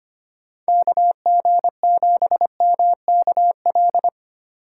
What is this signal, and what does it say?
Morse code, heard as a keyed tone, about 31 wpm, 700 Hz. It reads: KG7MKL